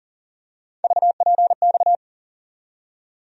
Morse code, heard as a keyed tone, about 40 wpm, 700 Hz. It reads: VPX
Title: USA Call Sign Postfixes - 3 Letters 40wpm